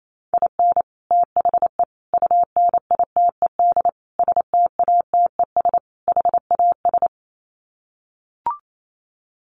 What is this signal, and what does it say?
Morse code, heard as a keyed tone, about 28 wpm, 700 Hz. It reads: IN THE UNITED STATES HAS
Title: Sets of 5 Words Encore2 28wpm